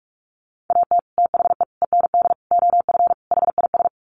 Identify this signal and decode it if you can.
Morse code, heard as a keyed tone, about 44 wpm, 700 Hz. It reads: AT THE END OF HIS
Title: Sets of 5 Words Encore 45wpm